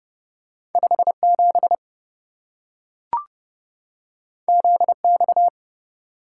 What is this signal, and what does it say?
Morse code, heard as a keyed tone, about 26 wpm, 700 Hz. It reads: 57 ZX